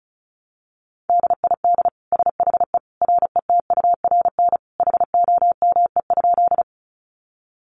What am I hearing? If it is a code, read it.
Morse code, heard as a keyed tone, about 34 wpm, 700 Hz. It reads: DID SHE RETURN HOME?
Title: Sentences from Top 800 Words 35wpm